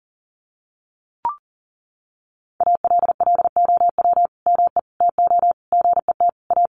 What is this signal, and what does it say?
Morse code, heard as a keyed tone, about 40 wpm, 700 Hz. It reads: ALLOW ME TO GET A
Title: Sentences from Top 1300 Words 40wpm